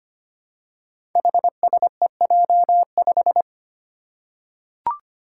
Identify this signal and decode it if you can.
Morse code, heard as a keyed tone, about 25 wpm, 700 Hz. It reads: HSEJ5
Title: Five Challenging Letter-Number Combinations 25wpm